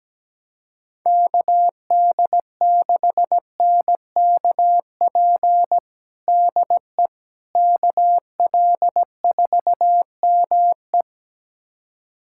Morse code, heard as a keyed tone, about 17 wpm, 700 Hz. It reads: KD6NKP DE KL4ME
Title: USA Callsign DE Callsign 17wpm